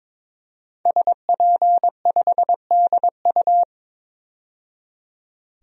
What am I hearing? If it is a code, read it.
Morse code, heard as a keyed tone, about 21 wpm, 700 Hz. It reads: SP5DU
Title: DXCC-100 Call Signs - 5 Characters - 22wpm